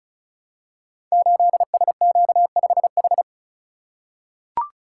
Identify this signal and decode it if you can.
Morse code, heard as a keyed tone, about 36 wpm, 700 Hz. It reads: 8SQ5H
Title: Code Groups - 5 Letter:Numbers - 35wpm